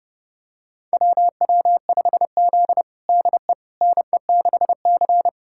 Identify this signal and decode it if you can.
Morse code, heard as a keyed tone, about 30 wpm, 700 Hz. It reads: WW5Z DE NE6C
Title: USA Callsign DE Callsign - 4 Characters - 30wpm